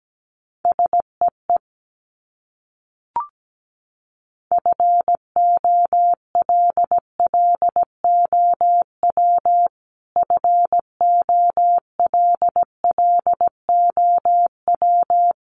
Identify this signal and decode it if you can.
Morse code, heard as a keyed tone, about 17 wpm, 700 Hz. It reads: SEE FOLLOW FOLLOW
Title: Top 500 Words Repeated 17wpm